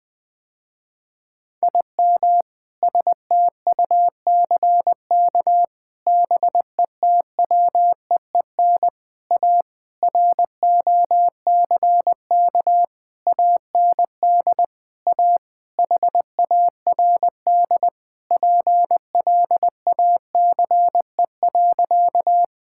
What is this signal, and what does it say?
Morse code, heard as a keyed tone, about 21 wpm, 700 Hz. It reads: IM STUCK BETWEEN A ROCK AND A HARD PLACE.
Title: Sentences with 7-Word Phrases - Spoken Beginning - 20wpm